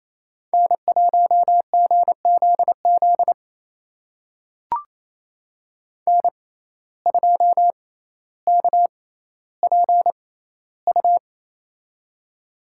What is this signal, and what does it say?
Morse code, heard as a keyed tone, about 28 wpm, spaced as for 14 wpm, 700 Hz. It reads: N1GZZ N2KPU
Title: USA Call Signs - 5 Characters - 6x Character Spacing - 28wpm